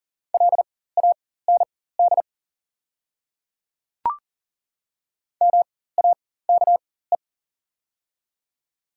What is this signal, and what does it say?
Morse code, heard as a keyed tone, about 40 wpm, spaced as for 19 wpm, 700 Hz. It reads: LAND MAKE
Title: 4x Character Spacing - Top 300 Words - 40wpm